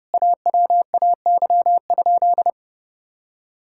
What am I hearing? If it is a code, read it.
Morse code, heard as a keyed tone, about 30 wpm, 700 Hz. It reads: AWAY?
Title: Sentences from Top 1000 Words - Review Part 6 - 30wpm